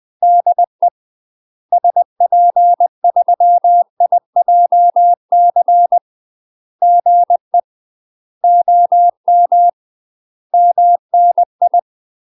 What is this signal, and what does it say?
Morse code, heard as a keyed tone, about 20 wpm, 700 Hz. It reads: DE SP3IJC GE OM MNI
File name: UK MASTER QSOs TRAINING / 20WPM / UK MASTER QSOs 0051